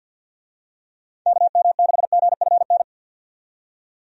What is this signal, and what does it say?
Morse code, heard as a keyed tone, about 51 wpm, 700 Hz. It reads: KM6GFN
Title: USA Call Signs - 6 Characters - 50wpm